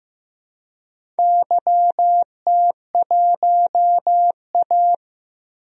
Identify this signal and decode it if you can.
Morse code, heard as a keyed tone, about 15 wpm, 700 Hz. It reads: YT1A
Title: DXCC-100 Call Signs - 4 Characters - 15wpm